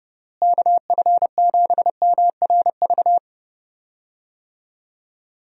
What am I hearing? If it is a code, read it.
Morse code, heard as a keyed tone, about 32 wpm, 700 Hz. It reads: KF7MRV
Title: USA Call Signs - 6 Characters - Repeated 2x - 30wpm